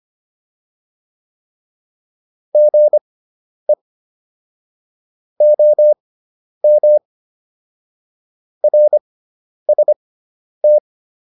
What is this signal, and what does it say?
Morse code, heard as a keyed tone, about 25 wpm, spaced as for 10 wpm, 600 Hz. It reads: GE OM RST
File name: QSO210 10